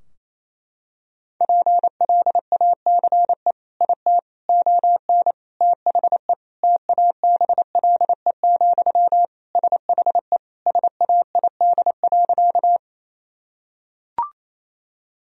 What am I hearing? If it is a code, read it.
Morse code, heard as a keyed tone, about 28 wpm, 700 Hz. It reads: PLACE IT ON THE TABLE, SHE SAID.
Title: Sentences from Top 1000 Words - Review Part 2 - 28wpm